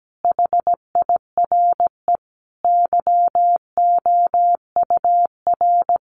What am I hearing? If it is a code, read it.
Morse code, heard as a keyed tone, about 17 wpm, 700 Hz. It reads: HIRE YOUR
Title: Sentences from Top 1400 Words 17wpm